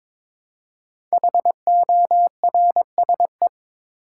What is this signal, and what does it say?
Morse code, heard as a keyed tone, about 22 wpm, 700 Hz. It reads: HORSE
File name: 5 Letter Words - First 3 Letters Given - 22wpm